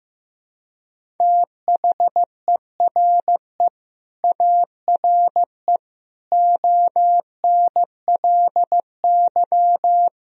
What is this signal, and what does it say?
Morse code, heard as a keyed tone, about 15 wpm, 700 Hz. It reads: THERE ARE ONLY
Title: Sentences from Top 1000 Words - Review Part 1 - 15wpm